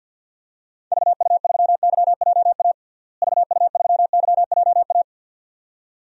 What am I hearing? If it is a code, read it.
Morse code, heard as a keyed tone, about 46 wpm, 700 Hz. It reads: VU3YJA VU3YJA
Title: DXCC-100 Call Signs - 6 Characters - Repeated 2x - 50wpm